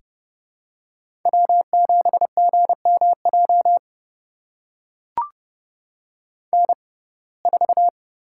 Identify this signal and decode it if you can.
Morse code, heard as a keyed tone, about 32 wpm, spaced as for 22 wpm, 700 Hz. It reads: W7GMJ N4
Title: USA Call Signs - 5 Characters - 6x Character Spacing - 30wpm